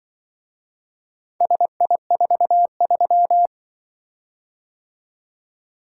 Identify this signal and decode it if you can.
Morse code, heard as a keyed tone, about 24 wpm, 700 Hz. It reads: SI43
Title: Speed-Racing - Four Letter-Number Combinations - 20wpm